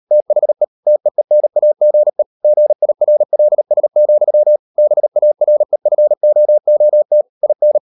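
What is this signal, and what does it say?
Morse code, heard as a keyed tone, about 38 wpm, 600 Hz. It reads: THE TEENAGE GIRLS, BAREFOOT IN
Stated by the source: Morse code practice at 38 WPM, sending Ken Follett's novel Never in English, from CW med Gnister.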